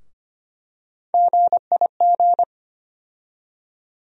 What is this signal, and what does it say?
Morse code, heard as a keyed tone, about 25 wpm, 700 Hz. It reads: GIG
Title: USA Call Sign Postfixes - 3 Letters 25wpm